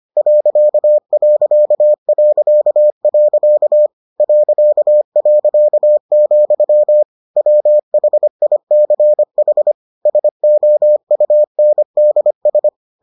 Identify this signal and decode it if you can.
Morse code, heard as a keyed tone, about 25 wpm, 600 Hz. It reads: .... .., WHICH SOUNDS